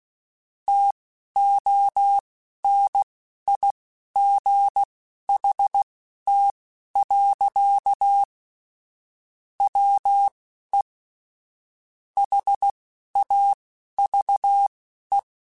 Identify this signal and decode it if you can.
Morse code, heard as a keyed tone, about 16 wpm, spaced as for 11 wpm, 800 Hz. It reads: TONIGHT. WE HAVE
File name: QSOs / QSOs-mp3s / WD0BGP 1@13